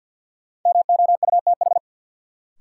Morse code, heard as a keyed tone, about 53 wpm, 700 Hz. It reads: MOUTH
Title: Sentences from Top 1500 Words - New Word Drill - 50wpm